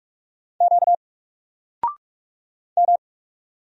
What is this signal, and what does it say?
Morse code, heard as a keyed tone, about 44 wpm, 700 Hz. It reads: Q M
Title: Single Letter-Number Alt 45wpm